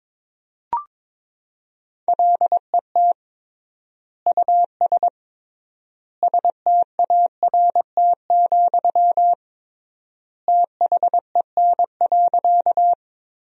Morse code, heard as a keyed tone, about 22 wpm, 700 Hz. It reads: LET US START, THEN.
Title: 3x Word-Spacing - Sentences from Top 300 Words - 22wpm